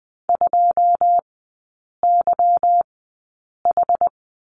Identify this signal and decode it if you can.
Morse code, heard as a keyed tone, about 20 wpm, spaced as for 8 wpm, 690 Hz. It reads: 2YH
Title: 166 Lesson 27e - Number 7 - Three Characters 20wpm